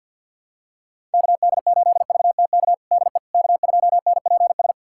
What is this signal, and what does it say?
Morse code, heard as a keyed tone, about 48 wpm, 700 Hz. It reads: KD8VTK DE K2NPS
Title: USA Callsign DE Callsign 50wpm